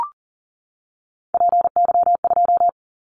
Morse code, heard as a keyed tone, about 40 wpm, 690 Hz. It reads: PY2
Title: Callsign Prefixes for Top 100 DXCC - 40wpm